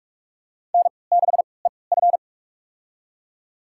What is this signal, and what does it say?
Morse code, heard as a keyed tone, about 46 wpm, spaced as for 26 wpm, 700 Hz. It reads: N6EF
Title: USA Call Signs - 3x Character Spacing - 45wpm